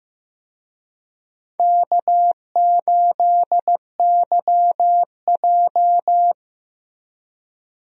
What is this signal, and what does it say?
Morse code, heard as a keyed tone, about 15 wpm, 700 Hz. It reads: K8YJ
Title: USA Call Signs - 4 Characters - Repeated 3x - 15wpm